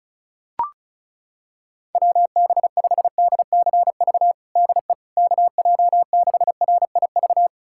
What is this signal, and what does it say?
Morse code, heard as a keyed tone, about 34 wpm, 700 Hz. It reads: WB5DCV DE KJ6RIV
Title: USA Callsign DE Callsign - 6 Characters - 35wpm